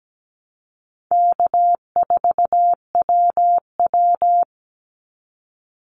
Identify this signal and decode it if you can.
Morse code, heard as a keyed tone, about 17 wpm, 700 Hz. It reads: K4WW
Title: Easy Call Signs Repeated 2x - 17wpm